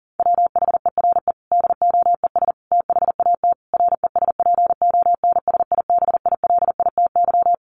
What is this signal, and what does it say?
Morse code, heard as a keyed tone, about 40 wpm, 700 Hz. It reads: WHERE DOES THAT RESPONSIBILITY
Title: Sentences from Top 1200 Words 40wpm